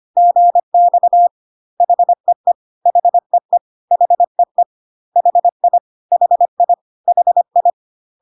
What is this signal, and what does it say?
Morse code, heard as a keyed tone, about 25 wpm, 700 Hz. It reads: GX HEE HEE HEE HI HI HI
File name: CW abbreviations alphabetical repeated 3 times 25 25wpm